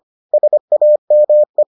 Morse code, heard as a keyed tone, about 25 wpm, 600 Hz. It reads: SAME